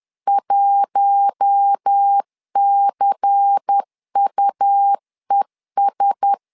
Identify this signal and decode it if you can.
Morse code, heard as a keyed tone, about 10 wpm, 800 Hz. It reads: 1CUES
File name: LEZIONI / 5 LBVC / 5 LBVC 10wpm 2349